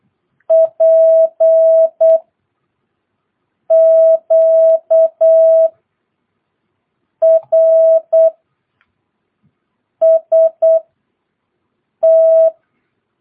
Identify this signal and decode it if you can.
Morse code, heard as a keyed tone, about 8 wpm, spaced as for 4 wpm, 650 Hz. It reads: PQRST
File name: jh8cbh / 2022cw